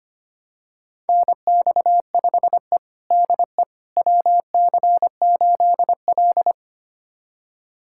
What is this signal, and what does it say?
Morse code, heard as a keyed tone, about 25 wpm, 700 Hz. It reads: NX5E DE WC8L
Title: USA Callsign DE Callsign - 4 Characters - 25wpm